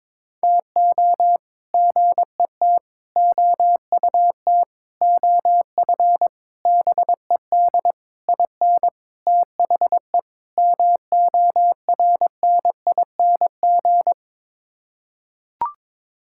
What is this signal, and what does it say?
Morse code, heard as a keyed tone, about 22 wpm, 700 Hz. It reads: TO GET OUT OF BED IN THE MORNING
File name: practice / 22wpm / Sets of 8 Words - 22wpm